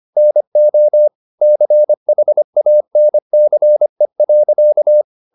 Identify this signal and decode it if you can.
Morse code, heard as a keyed tone, about 25 wpm, 600 Hz. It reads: NO CHANCE.
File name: the old man and the sea part 1 25wpm